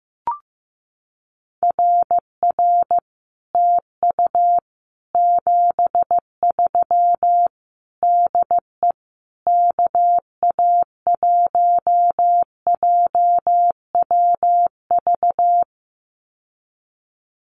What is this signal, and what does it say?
Morse code, heard as a keyed tone, about 15 wpm, 700 Hz. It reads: RR TU 73 DE KA1JWV
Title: Common QSO Phrases Encore 15wpm